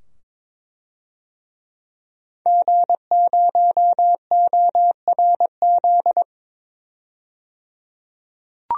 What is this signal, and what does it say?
Morse code, heard as a keyed tone, about 23 wpm, 700 Hz. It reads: G0ORZ
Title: DXCC-100 Call Signs - 5 Characters - 22wpm